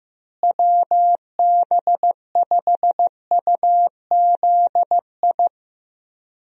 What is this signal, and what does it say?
Morse code, heard as a keyed tone, about 15 wpm, 700 Hz. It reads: WB5UZI
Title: USA Call Signs - 6 Characters - Repeated 3x - 15wpm